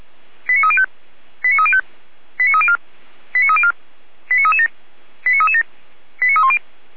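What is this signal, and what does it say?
Five-tone selective call (ZVEI), heard as quick runs of five tones, tones of about 80 ms. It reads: 89305 89305 89304 89304 89337 89337 89310
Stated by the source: Five-tone ZVEI selective call received on an SDR, from feuerwehr-orlamuende.org (fire brigade Orlamünde).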